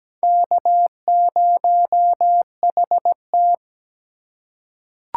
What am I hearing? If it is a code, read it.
Morse code, heard as a keyed tone, about 17 wpm, 700 Hz. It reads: K0HT
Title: K1USN SST QSO Practice - 17wpm